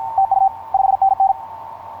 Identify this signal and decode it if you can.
Morse code, heard as a keyed tone, about 44 wpm, 780 Hz, with radio noise. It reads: TU 5NN
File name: yj0ha ha8be etc 15m cw 10nov2010 0839